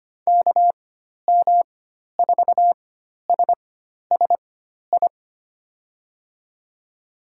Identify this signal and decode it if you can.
Morse code, heard as a keyed tone, about 25 wpm, spaced as for 12 wpm, 700 Hz. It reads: KM4SSI
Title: USA Call Signs - 6 Characters - 4x Character Spacing - 25wpm